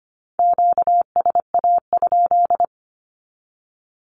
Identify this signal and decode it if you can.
Morse code, heard as a keyed tone, about 25 wpm, 700 Hz. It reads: QSA?
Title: Q-Codes with Spoken Meaning - 25wpm